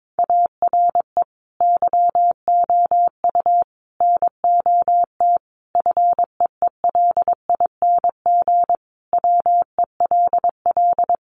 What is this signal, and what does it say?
Morse code, heard as a keyed tone, about 22 wpm, 700 Hz. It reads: ARE YOU NOT FEELING WELL
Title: Sentences from Top 1200 Words 22wpm